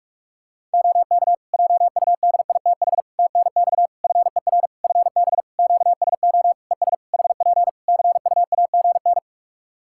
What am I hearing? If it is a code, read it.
Morse code, heard as a keyed tone, about 44 wpm, 700 Hz. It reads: OK JUDITH TNX FER FB QSO ES HP CUAGN
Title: Practice QSO - Exchange 4 - Ending - 45wpm